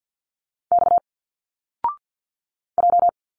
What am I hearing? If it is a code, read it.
Morse code, heard as a keyed tone, about 52 wpm, 700 Hz. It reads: X J
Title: Single Letters 50wpm